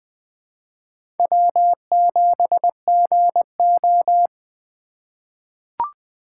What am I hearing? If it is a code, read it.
Morse code, heard as a keyed tone, about 21 wpm, 700 Hz. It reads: W7GO
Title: USA Call Signs - 6x Character Spacing - 20wpm